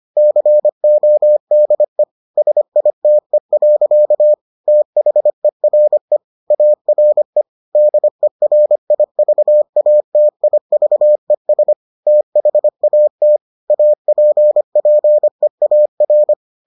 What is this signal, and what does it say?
Morse code, heard as a keyed tone, about 25 wpm, 600 Hz. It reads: CODE SITE. THERE ARE DERIVATIVES THAT APPEAR